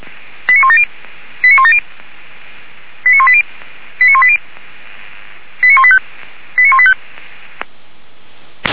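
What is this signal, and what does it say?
Five-tone selective call (ZVEI), heard as quick runs of five tones, tones of about 70 ms. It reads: 89170 89170 89180 89180 89185 89185
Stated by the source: ZVEI five-tone call 89135 received by SDR, recorded by Feuerwehr Orlamünde server.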